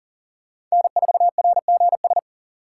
Morse code, heard as a keyed tone, about 37 wpm, 700 Hz. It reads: N4RGS
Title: USA Call Signs - 5 Characters - 3x Character Spacing - 40wpm